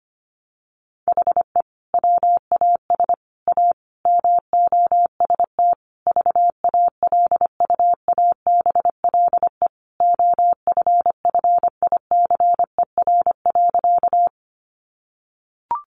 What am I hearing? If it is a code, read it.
Morse code, heard as a keyed tone, about 25 wpm, 700 Hz. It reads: HE WAS A MOST VALUABLE OFFICER.
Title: Sentences from Top 1200 Words 25wpm